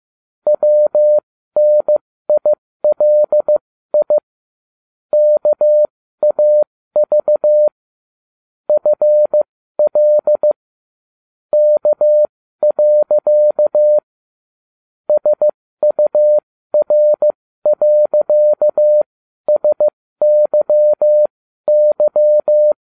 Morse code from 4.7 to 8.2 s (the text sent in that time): KAV